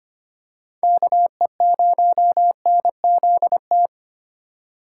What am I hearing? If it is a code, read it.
Morse code, heard as a keyed tone, about 25 wpm, 700 Hz. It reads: KE0NZT
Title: USA Call Signs - 6 Characters - 4x Character Spacing - 25wpm